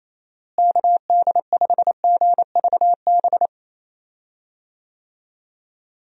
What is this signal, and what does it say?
Morse code, heard as a keyed tone, about 28 wpm, 700 Hz. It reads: KD5GVB
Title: USA Call Signs - 6 Characters - Repeated 2x - 28wpm